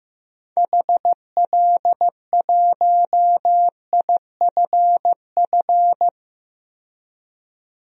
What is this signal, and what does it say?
Morse code, heard as a keyed tone, about 15 wpm, 700 Hz. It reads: HL1IFF
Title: DXCC-100 Call Signs - 6 Characters - 15wpm